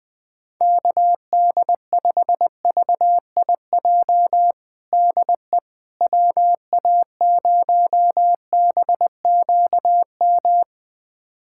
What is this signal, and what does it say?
Morse code, heard as a keyed tone, about 21 wpm, 700 Hz. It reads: KD5VIJ DE WA0BQM
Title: USA Callsign DE Callsign - 6 Characters - 20wpm